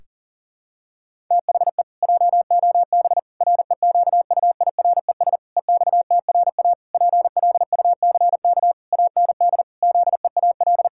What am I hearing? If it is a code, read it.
Morse code, heard as a keyed tone, about 40 wpm, 700 Hz. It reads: THE JOB REQUIRES EXTRA PLUCK AND ZEAL
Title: Pangram Challenge v2 - 40wpm